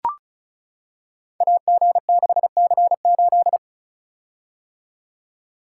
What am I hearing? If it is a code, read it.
Morse code, heard as a keyed tone, about 36 wpm, 690 Hz. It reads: AG6C8
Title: Code Groups - 5 Letter:Numbers - 35wpm